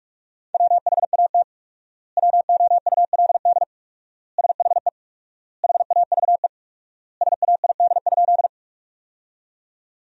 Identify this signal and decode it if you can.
Morse code, heard as a keyed tone, about 43 wpm, 700 Hz. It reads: WHAT WOULD SHE HAVE SAID?
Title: 4x Word-Spacing - Sentences from Top 100 Words - 45wpm